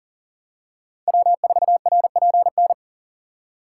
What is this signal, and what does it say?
Morse code, heard as a keyed tone, about 37 wpm, 700 Hz. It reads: W4RPN
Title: USA Call Signs - 5 Characters - 2x Character Spacing - 40wpm